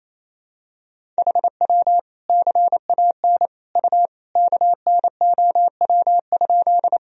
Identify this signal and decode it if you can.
Morse code, heard as a keyed tone, about 27 wpm, 700 Hz. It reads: HW CAN U KNOW?
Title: Sentences from Top 100 Words and Abbreviations 28wpm